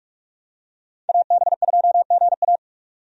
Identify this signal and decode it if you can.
Morse code, heard as a keyed tone, about 46 wpm, 700 Hz. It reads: AB2GA
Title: USA Call Signs - 5 Characters - 45wpm